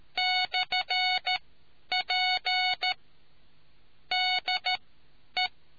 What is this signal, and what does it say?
Morse code, heard as a keyed tone, about 13 wpm, spaced as for 9 wpm, 740 Hz. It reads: /P DE